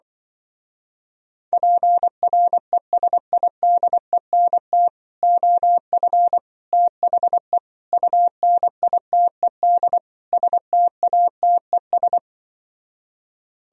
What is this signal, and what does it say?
Morse code, heard as a keyed tone, about 24 wpm, 700 Hz. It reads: PRESIDENT OF THE UNITED STATES
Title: Speed-Racing - Sets of 5 Words 20wpm